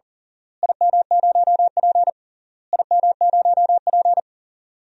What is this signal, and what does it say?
Morse code, heard as a keyed tone, about 37 wpm, 700 Hz. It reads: IM0P IM0P
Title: DXCC-100 Call Signs - 4 Characters - Repeated 3x - 40wpm